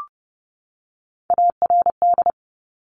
Morse code, heard as a keyed tone, about 30 wpm, 700 Hz. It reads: ARD
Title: Top Three Letter Combinations in Words 30wpm